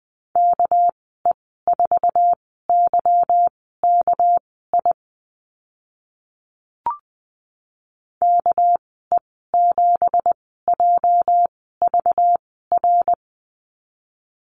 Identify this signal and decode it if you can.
Morse code, heard as a keyed tone, about 20 wpm, spaced as for 14 wpm, 700 Hz. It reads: KE4YKI KE7JVR
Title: 213 Lesson 34g - Number 8 - Call Signs 20wpm